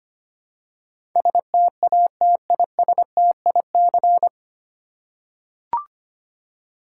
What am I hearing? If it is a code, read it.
Morse code, heard as a keyed tone, about 25 wpm, 700 Hz. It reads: STATISTIC
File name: Nine Letter Words - 25wpm